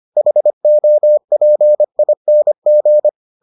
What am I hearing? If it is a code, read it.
Morse code, heard as a keyed tone, about 25 wpm, 600 Hz. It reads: HOPING